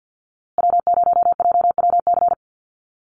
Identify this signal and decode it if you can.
Morse code, heard as a keyed tone, about 50 wpm, 700 Hz. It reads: W0JWC
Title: USA Call Signs Repeated 2x 50wpm